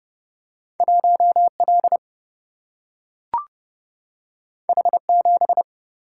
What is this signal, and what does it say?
Morse code, heard as a keyed tone, about 31 wpm, 700 Hz. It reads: 1 L H 7
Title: Single-Letter Single-Number 30wpm